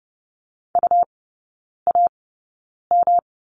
Morse code, heard as a keyed tone, about 30 wpm, spaced as for 9 wpm, 700 Hz. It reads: UAM
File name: Lesson 18e - Letter m - Three Characters 30wpm